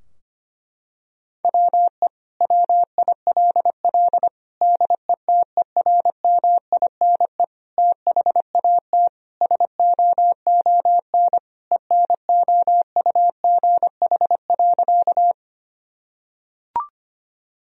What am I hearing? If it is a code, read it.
Morse code, heard as a keyed tone, about 25 wpm, 700 Hz. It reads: WE WILL DETERMINE THAT SOON ENOUGH.
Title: Sentences from Top 1000 Words - Review Part 4 - 25wpm